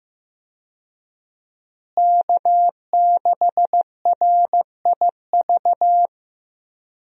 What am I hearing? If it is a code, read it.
Morse code, heard as a keyed tone, about 15 wpm, 700 Hz. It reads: K6RIV
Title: USA Call Signs - 5 Characters - Repeated 3x - 15wpm